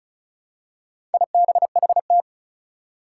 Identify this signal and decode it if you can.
Morse code, heard as a keyed tone, about 36 wpm, 700 Hz. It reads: IBHT
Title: Four Challenging Letter-Number Combinations 35wpm